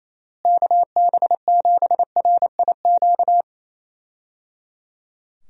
Morse code, heard as a keyed tone, about 28 wpm, 700 Hz. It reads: KB7RIQ
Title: USA Call Signs - 6 Characters - Repeated 3x - 28wpm